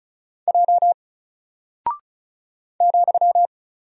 Morse code, heard as a keyed tone, about 36 wpm, 700 Hz. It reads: J ,
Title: Single Character (Letter:Number:Punctuation:Prosign) Alt 35wpm